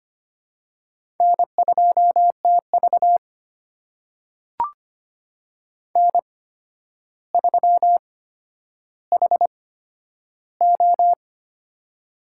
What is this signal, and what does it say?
Morse code, heard as a keyed tone, about 25 wpm, spaced as for 11 wpm, 700 Hz. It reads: N2TV N3HO
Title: USA Call Signs - 4 Characters - 8x Character Spacing - 25wpm